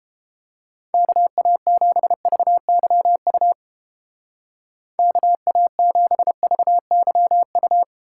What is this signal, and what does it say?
Morse code, heard as a keyed tone, about 29 wpm, 700 Hz. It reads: KA7VYU KA7VYU